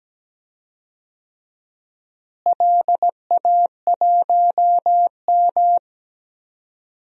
Morse code, heard as a keyed tone, about 17 wpm, 700 Hz. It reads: LA1M
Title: DXCC-100 Call Signs - 4 Characters - Repeated 3x - 17wpm